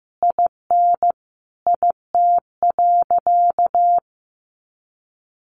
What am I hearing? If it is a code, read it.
Morse code, heard as a keyed tone, about 15 wpm, 700 Hz. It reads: IN IT.
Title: Sentences from Top 1100 Words 15wpm